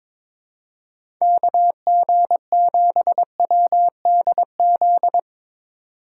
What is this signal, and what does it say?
Morse code, heard as a keyed tone, about 22 wpm, 700 Hz. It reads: KG7WDZ
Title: USA Call Signs - 6 Characters - 8x Character Spacing - 22wpm